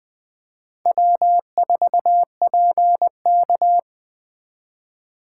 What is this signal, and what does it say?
Morse code, heard as a keyed tone, about 20 wpm, 700 Hz. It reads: W4PK
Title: Speed Racing - USA 1x2 Call Signs 17wpm